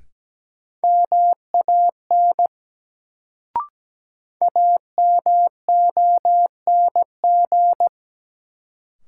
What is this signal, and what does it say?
Morse code, heard as a keyed tone, about 17 wpm, 700 Hz. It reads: MAN AMONG
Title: Sets of 2 Words - New Word Drill - 17wpm